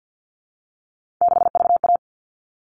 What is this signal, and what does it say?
Morse code, heard as a keyed tone, about 51 wpm, 700 Hz. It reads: 6VA
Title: All Cut Numbers - 3-Digit 50wpm